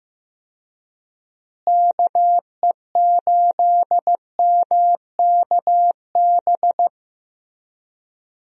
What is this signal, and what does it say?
Morse code, heard as a keyed tone, about 15 wpm, 700 Hz. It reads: KE8MKB
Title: USA Call Signs - 6 Characters - Repeated 2x - 15wpm